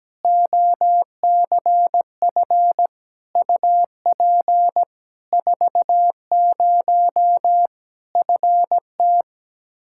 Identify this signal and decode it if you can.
Morse code, heard as a keyed tone, about 17 wpm, 700 Hz. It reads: OCF UP 40 FT
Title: K1USN SST QSO Practice 17wpm